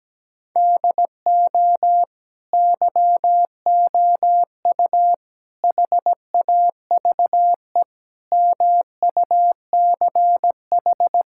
Morse code, heard as a keyed tone, about 17 wpm, 700 Hz. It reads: DO YOU HAVE MUCH
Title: Sentences from Top 1000 Words - Review Part 6 - 17wpm